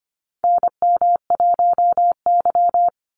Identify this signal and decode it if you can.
Morse code, heard as a keyed tone, about 25 wpm, 700 Hz. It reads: NM1Y
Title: Call Signs - 25wpm